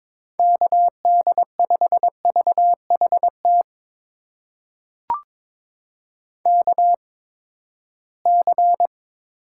Morse code, heard as a keyed tone, about 22 wpm, spaced as for 15 wpm, 700 Hz. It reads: KD5VHT KC
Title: USA Call Signs - 6 Characters - 8x Character Spacing - 22wpm